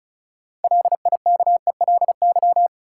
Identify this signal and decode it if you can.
Morse code, heard as a keyed tone, about 36 wpm, 700 Hz. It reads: LIKELY
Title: Sentences from Top 600 Words - New Word Drill - 35wpm